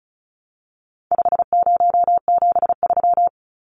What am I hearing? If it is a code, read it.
Morse code, heard as a keyed tone, about 34 wpm, 700 Hz. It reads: 5073
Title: Numbers - 4-Digit 35wpm